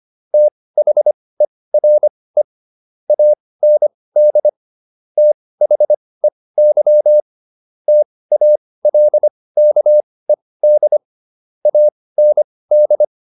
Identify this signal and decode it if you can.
Morse code, heard as a keyed tone, about 25 wpm, spaced as for 18 wpm, 600 Hz. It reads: THERE AND THEY TALKED AND